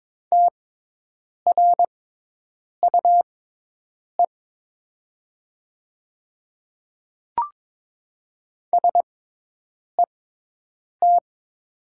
Morse code, heard as a keyed tone, about 22 wpm, spaced as for 8 wpm, 700 Hz. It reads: TRUE SET
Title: 6x Character Spacing - Top 500 Words - 22wpm